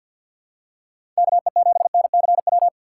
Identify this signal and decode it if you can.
Morse code, heard as a keyed tone, about 52 wpm, 700 Hz. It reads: KE7NCW
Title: USA Call Signs - 6 Characters - 3x Character Spacing - 50wpm